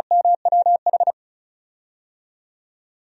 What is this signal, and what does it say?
Morse code, heard as a keyed tone, about 34 wpm, 700 Hz. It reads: MWH
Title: DXCC-100 Call Signs - 6 Characters - Repeated 2x - 35wpm